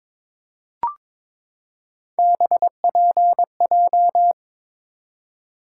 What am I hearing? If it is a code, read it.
Morse code, heard as a keyed tone, about 22 wpm, 700 Hz. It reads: BPJ
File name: practice / 22wpm / Three Letter-Number Combinations - 22wpm